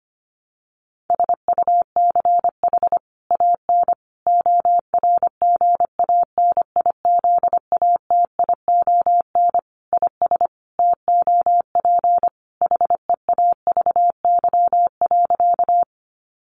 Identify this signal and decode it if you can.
Morse code, heard as a keyed tone, about 25 wpm, 700 Hz. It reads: SUCH AN ORGANIZATION IS TOP HEAVY.
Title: Sentences from Top 1100 Words 25wpm